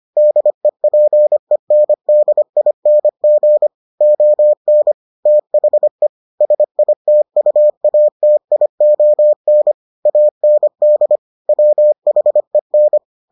Morse code, heard as a keyed tone, about 25 wpm, 600 Hz. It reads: DEPENDING ON THE SITUATION AND WHEN